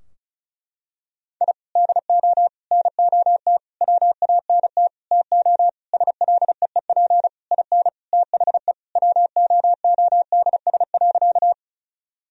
Morse code, heard as a keyed tone, about 36 wpm, 700 Hz. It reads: I DO NOT WANT TO SLEEP IN THE WOODS.
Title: Sentences from Top 1000 Words - Review Part 2 - 35wpm